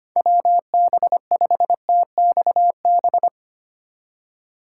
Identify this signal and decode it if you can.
Morse code, heard as a keyed tone, about 28 wpm, 700 Hz. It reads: WB5TXB
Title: Speed Racing - USA 2x3 Call Signs 25wpm